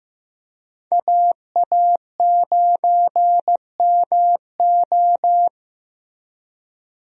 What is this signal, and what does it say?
Morse code, heard as a keyed tone, about 15 wpm, 700 Hz. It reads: AA9MO
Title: USA Call Signs - 5 Characters - Repeated 2x - 15wpm